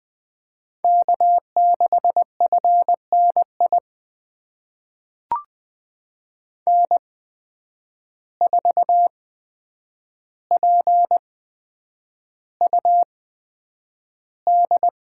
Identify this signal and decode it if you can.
Morse code, heard as a keyed tone, about 20 wpm, spaced as for 9 wpm, 700 Hz. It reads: K6FNI N4PUD
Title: USA Call Signs - 5 Characters - 8x Character Spacing - 20wpm